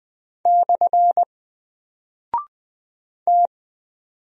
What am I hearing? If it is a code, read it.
Morse code, heard as a keyed tone, about 20 wpm, 700 Hz. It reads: / T
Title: Single Character (Letter:Number:Punctuation:Prosign) 20wpm